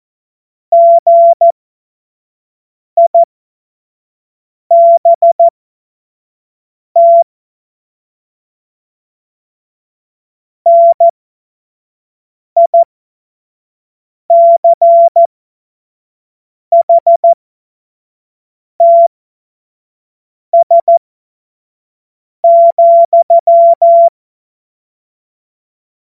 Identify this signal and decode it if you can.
Morse code, heard as a keyed tone, about 14 wpm, spaced as for 5 wpm, 680 Hz. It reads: GIBT NICHTS,